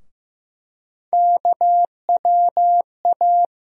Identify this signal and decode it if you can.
Morse code, heard as a keyed tone, about 15 wpm, 700 Hz. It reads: KWA
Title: USA Call Sign Postfixes - 3 Letters 15wpm